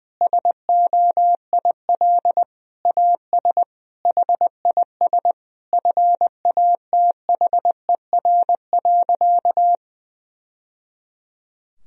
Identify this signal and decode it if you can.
Morse code, heard as a keyed tone, about 20 wpm, 700 Hz. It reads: SOIL AS HIS FATHER.
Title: Sentences from Top 1000 Words - Review Part 4 - 20wpm